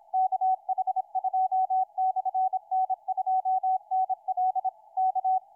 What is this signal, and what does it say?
Morse code, heard as a keyed tone, about 26 wpm, 750 Hz. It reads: KH2/N2NL K
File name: kh2n2nl 30-12-13 1241ut 160m